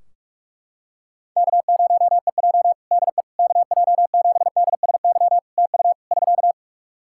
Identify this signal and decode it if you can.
Morse code, heard as a keyed tone, about 44 wpm, 700 Hz. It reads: K0EJ DE KJ7DSY TU <SK>
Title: Practice QSO - Exchange 4 - Ending - 45wpm